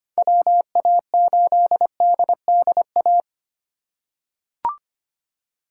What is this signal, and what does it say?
Morse code, heard as a keyed tone, about 25 wpm, 700 Hz. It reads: WA8DDA
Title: USA Call Signs - 6 Characters - 2x Character Spacing - 25wpm